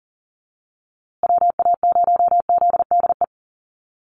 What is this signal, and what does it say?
Morse code, heard as a keyed tone, about 40 wpm, 700 Hz. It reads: WA0ZDE
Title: USA Call Signs Repeated 3x 40wpm